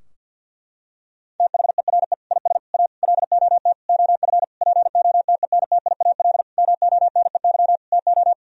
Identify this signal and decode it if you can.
Morse code, heard as a keyed tone, about 52 wpm, 700 Hz. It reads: THERE IS A LOT OF POTENTIAL MONEY TO
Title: Sentences from Top 1000 Words - Review Part 4 - 50wpm